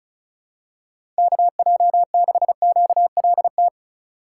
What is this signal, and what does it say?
Morse code, heard as a keyed tone, about 36 wpm, 700 Hz. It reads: KJ6QLT
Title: USA Call Signs - 6 Characters - 8x Character Spacing - 35wpm